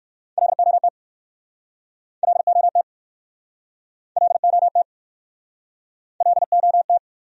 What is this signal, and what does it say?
Morse code, heard as a keyed tone, about 64 wpm, 700 Hz. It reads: LOT LOT LOT LOT